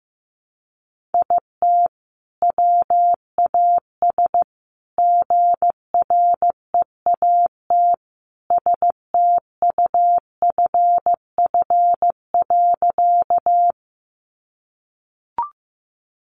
Morse code, heard as a keyed tone, about 15 wpm, 700 Hz. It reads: IT WAS GREAT STUFF.
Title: Sentences from Top 1400 Words 15wpm